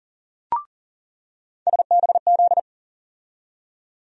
Morse code, heard as a keyed tone, about 40 wpm, 700 Hz. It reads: SBZ
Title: USA Call Sign Postfixes - 3 Letters 40wpm